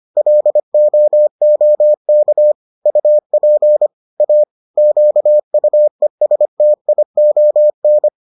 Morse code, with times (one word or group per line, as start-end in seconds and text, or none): 0.2-2.5 LOOK
2.9-3.9 UP
4.2-4.4 A
4.8-8.1 QUESTION